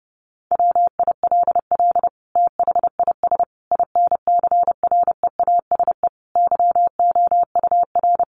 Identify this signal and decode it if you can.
Morse code, heard as a keyed tone, about 30 wpm, 700 Hz. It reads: WILL THIS INCREASE YOUR
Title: Sentences from Top 1200 Words 30wpm